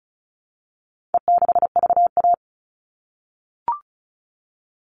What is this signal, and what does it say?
Morse code, heard as a keyed tone, about 34 wpm, 700 Hz. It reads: E6VA
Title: All Cut Numbers - 4-Digit 35wpm